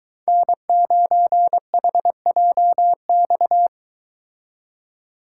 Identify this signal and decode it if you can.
Morse code, heard as a keyed tone, about 23 wpm, 700 Hz. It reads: N9HJX